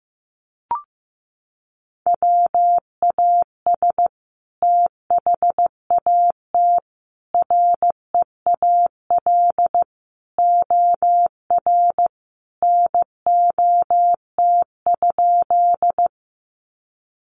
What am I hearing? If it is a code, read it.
Morse code, heard as a keyed tone, about 15 wpm, 700 Hz. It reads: WAS THAT REAL OR NOT?
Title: Sentences from Top 600 Words 15wpm